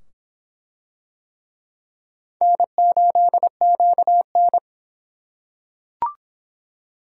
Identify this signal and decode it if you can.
Morse code, heard as a keyed tone, about 24 wpm, 700 Hz. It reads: N8QN